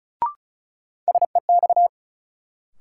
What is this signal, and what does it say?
Morse code, heard as a keyed tone, about 36 wpm, 700 Hz. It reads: SEX
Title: Sentences from Top 1600 Words - New Word Drill Alt - 35wpm